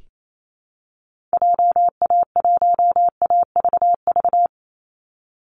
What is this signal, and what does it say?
Morse code, heard as a keyed tone, about 28 wpm, 700 Hz. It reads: JA1AVV
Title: USA Call Signs 28wpm